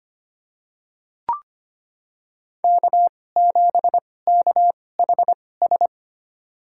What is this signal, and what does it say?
Morse code, heard as a keyed tone, about 25 wpm, spaced as for 18 wpm, 700 Hz. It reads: K7KHS
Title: USA Call Signs - 2x Character Spacing - 25wpm